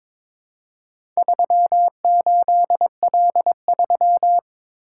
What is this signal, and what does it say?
Morse code, heard as a keyed tone, about 22 wpm, 700 Hz. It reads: 38L3
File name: practice / 22wpm / Four Letter-Number Combinations - 22wpm